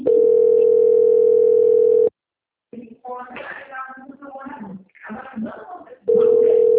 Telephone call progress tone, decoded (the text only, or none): ringback tone